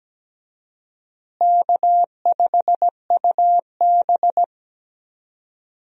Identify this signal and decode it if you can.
Morse code, heard as a keyed tone, about 17 wpm, 700 Hz. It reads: K5UB
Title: USA Call Signs - 4 Characters - Repeated 2x - 17wpm